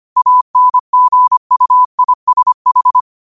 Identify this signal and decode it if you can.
Morse code, heard as a keyed tone, about 25 wpm, 1000 Hz. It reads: ANGUISH
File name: qotdfiles / 25wpm